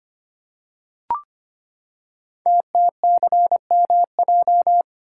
Morse code, heard as a keyed tone, about 25 wpm, 700 Hz. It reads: TTCMJ
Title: Code Groups - 5 Letters - 25wpm